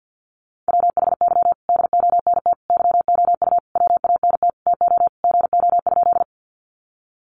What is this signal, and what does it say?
Morse code, heard as a keyed tone, about 48 wpm, 700 Hz. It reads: WHY DONT YOU WANT TO GO?
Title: Sentences from Top 300 Words 50wpm